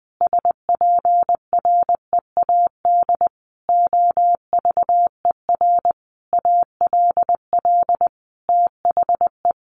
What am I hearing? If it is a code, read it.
Morse code, heard as a keyed tone, about 20 wpm, 700 Hz. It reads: SPREAD OVER ALL THE
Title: Sentences from Top 1200 Words 20wpm